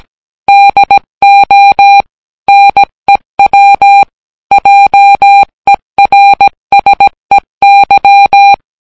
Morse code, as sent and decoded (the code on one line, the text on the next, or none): -.. --- / -. .-- / .--- . .-. ... -.--
DO NEW JERSEY